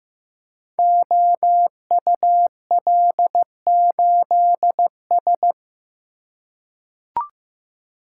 Morse code, heard as a keyed tone, about 15 wpm, 700 Hz. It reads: OUL8S
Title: Code Groups - 5 Letter:Numbers - 15wpm